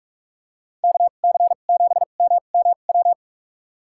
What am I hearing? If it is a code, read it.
Morse code, heard as a keyed tone, about 43 wpm, spaced as for 32 wpm, 700 Hz. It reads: KC7MMW
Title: USA Call Signs - 6 Characters - 2x Character Spacing - 45wpm